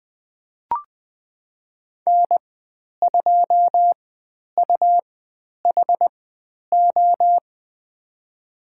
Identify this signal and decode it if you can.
Morse code, heard as a keyed tone, about 20 wpm, spaced as for 10 wpm, 700 Hz. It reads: N2UHO
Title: 102 Lesson 17g - Letter u - Call Signs 20wpm